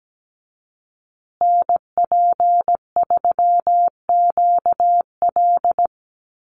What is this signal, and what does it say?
Morse code, heard as a keyed tone, about 17 wpm, 700 Hz. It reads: NP3QL
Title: USA Call Signs Repeated 3x 17wpm